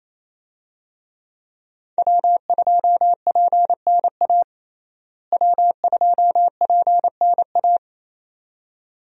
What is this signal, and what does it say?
Morse code, heard as a keyed tone, about 27 wpm, 700 Hz. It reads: W2PNA W2PNA
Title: USA Call Signs - 5 Characters - Repeated 2x - 28wpm